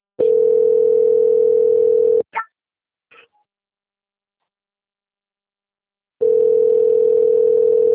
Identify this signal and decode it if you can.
Ringback tone (North America), 440 + 480 Hz.
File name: out- - - - - .1924